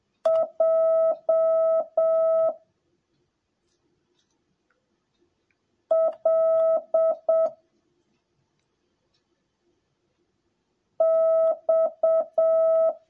Morse code, heard as a keyed tone, about 7 wpm, 650 Hz. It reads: J L X
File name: jh8cbh / 2021cw / cw10a